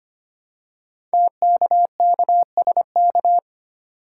Morse code, heard as a keyed tone, about 25 wpm, 700 Hz. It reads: TKKSK
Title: Code Groups - 5 Letters - 25wpm